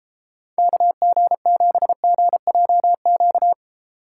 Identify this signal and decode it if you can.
Morse code, heard as a keyed tone, about 29 wpm, 700 Hz. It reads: KG7GJQ